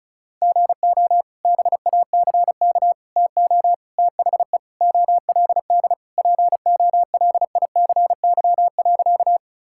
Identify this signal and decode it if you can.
Morse code, heard as a keyed tone, about 34 wpm, 700 Hz. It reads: GO BACK TO THE OLD POLICY.
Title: Sentences from Top 1000 Words - Review Part 5 - 35wpm